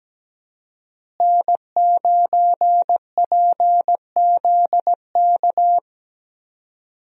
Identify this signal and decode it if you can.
Morse code, heard as a keyed tone, about 17 wpm, 700 Hz. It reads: N9PZK
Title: USA Call Signs - 5 Characters - Repeated 3x - 17wpm